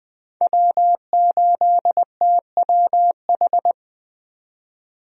Morse code, heard as a keyed tone, about 20 wpm, 700 Hz. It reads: W8TWH
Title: USA Call Signs - 5 Characters - 20wpm